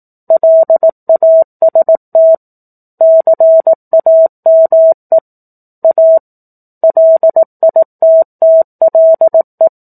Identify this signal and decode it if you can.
Morse code, heard as a keyed tone, about 18 wpm, 650 Hz. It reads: LAST CAME A LITTLE
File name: 04-Chapter-04 18 WPM